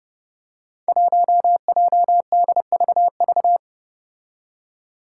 Morse code, heard as a keyed tone, about 29 wpm, 700 Hz. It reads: 1JDVV
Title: Five Challenging Letter-Number Combinations 30wpm